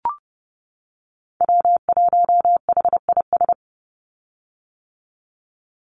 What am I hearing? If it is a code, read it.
Morse code, heard as a keyed tone, about 30 wpm, 700 Hz. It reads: W1HIS
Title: Call Signs - 30wpm